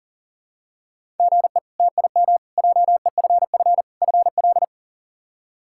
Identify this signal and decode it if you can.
Morse code, heard as a keyed tone, about 38 wpm, 700 Hz. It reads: GE TIM JEFF FL
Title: K1USN SST QSO Practice - 40wpm